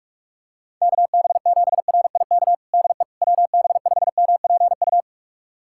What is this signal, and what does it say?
Morse code, heard as a keyed tone, about 46 wpm, 690 Hz. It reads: KB7RIK DE WB5MPU
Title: USA Callsign DE Callsign - 6 Characters - 45wpm